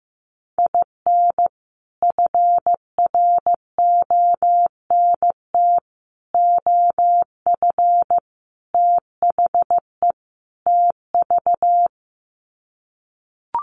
Morse code, heard as a keyed tone, about 15 wpm, 700 Hz. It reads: IN FRONT OF THE TV
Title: Sets of 5 Words Encore2 15wpm